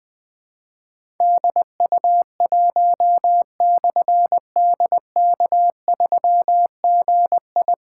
Morse code, heard as a keyed tone, about 20 wpm, 700 Hz. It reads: DU1/DK3GI
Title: Difficult Call Signs 20wpm